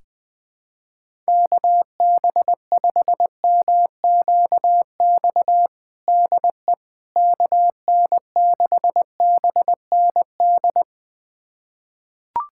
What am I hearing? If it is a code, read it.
Morse code, heard as a keyed tone, about 20 wpm, 700 Hz. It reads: KB5MQX DE KN6BND
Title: USA Callsign DE Callsign 20wpm